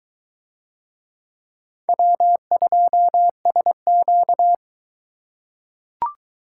Morse code, heard as a keyed tone, about 22 wpm, 700 Hz. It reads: W2SQ